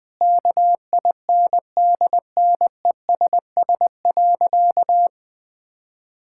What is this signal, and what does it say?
Morse code, heard as a keyed tone, about 20 wpm, 700 Hz. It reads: KINDNESS.
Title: Difficult Sentences with Hints - 20wpm